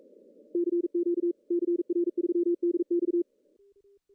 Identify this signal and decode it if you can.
Morse code, heard as a keyed tone, about 44 wpm, 380 Hz. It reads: CQ CR3DX
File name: CW / cqww-cw-k5tr